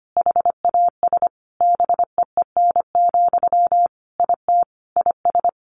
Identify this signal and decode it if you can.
Morse code, heard as a keyed tone, about 25 wpm, 700 Hz. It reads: HAS BEEN, IT IS
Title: Sentences from Top 1100 Words 25wpm